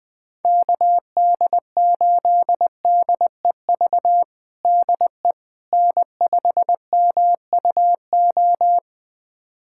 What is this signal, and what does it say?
Morse code, heard as a keyed tone, about 21 wpm, 700 Hz. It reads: KD8DEV DE N5MUO
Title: USA Callsign DE Callsign 20wpm